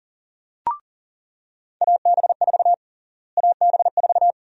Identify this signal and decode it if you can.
Morse code, heard as a keyed tone, about 40 wpm, 700 Hz. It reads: AB4 AB4
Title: USA Call Sign Prefixes - Repeated 2x - 40wpm